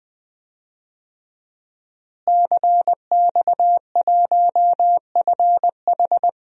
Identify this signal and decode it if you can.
Morse code, heard as a keyed tone, about 20 wpm, 700 Hz. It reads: CX1FH
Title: DXCC-100 Call Signs - 5 Characters - Repeated 2x - 20wpm